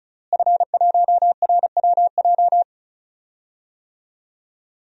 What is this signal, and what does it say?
Morse code, heard as a keyed tone, about 34 wpm, 700 Hz. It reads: F1RWJ
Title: DXCC-100 Call Signs - 5 Characters - Repeated 3x - 35wpm